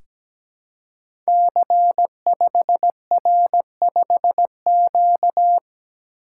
Morse code, heard as a keyed tone, about 17 wpm, 700 Hz. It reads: C5R5Q
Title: Code Groups - 5 Letter:Numbers - 17wpm